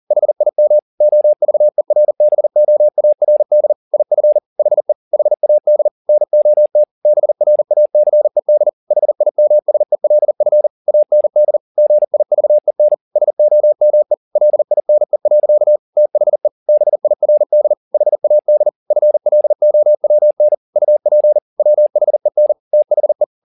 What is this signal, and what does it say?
Morse code, heard as a keyed tone, about 40 wpm, 600 Hz. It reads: HIM OVERBOARD IF HE HAD NOT BRACED HIMSELF AND GIVEN SOME LINE. THE BIRD HAD FLOWN UP WHEN THE